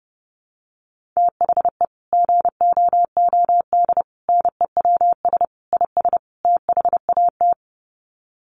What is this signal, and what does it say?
Morse code, heard as a keyed tone, about 30 wpm, 700 Hz. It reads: THE GOOD NEWS IS THAT
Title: Sets of 5 Words 30wpm